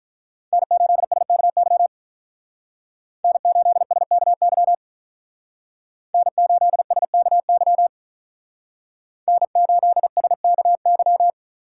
Morse code, heard as a keyed tone, about 45 wpm, 700 Hz. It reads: N8SKY N8SKY N8SKY N8SKY